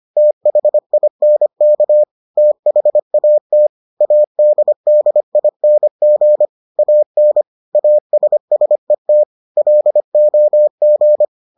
Morse code, heard as a keyed tone, about 25 wpm, 600 Hz. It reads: THINK THAT ADDING AN ASSET LOG